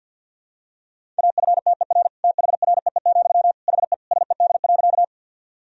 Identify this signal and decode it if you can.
Morse code, heard as a keyed tone, about 50 wpm, 700 Hz. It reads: AFTER THREE, HE SED.
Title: Sentences from Top 200 Words and Abbreviations 50wpm